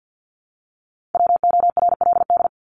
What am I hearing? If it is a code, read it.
Morse code, heard as a keyed tone, about 50 wpm, 700 Hz. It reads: WORLD
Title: Top 500 Words Repeated 50wpm